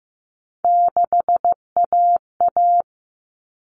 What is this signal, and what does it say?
Morse code, heard as a keyed tone, about 15 wpm, 700 Hz. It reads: 6AA
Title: All Cut Numbers - 3-Digit 15wpm